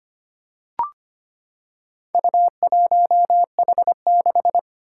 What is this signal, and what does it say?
Morse code, heard as a keyed tone, about 25 wpm, 700 Hz. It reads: U1H6
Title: Four Challenging Letter-Number Combinations 25wpm